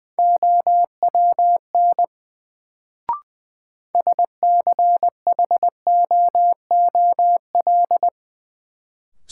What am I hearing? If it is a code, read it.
Morse code, heard as a keyed tone, about 21 wpm, 700 Hz. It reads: OWN SCHOOL
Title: Sentences from Top 300 Words - New Word Drill - 20wpm